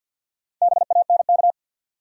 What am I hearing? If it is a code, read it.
Morse code, heard as a keyed tone, about 53 wpm, 700 Hz. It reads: BANK
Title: Sentences from Top 900 Words - New Word Drill - 50wpm